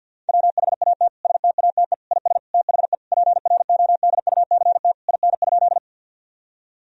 Morse code, heard as a keyed tone, about 50 wpm, 700 Hz. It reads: WHAT STATE IS THE PRODUCT IN?
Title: Sentences from Top 1000 Words - Review Part 2 - 50wpm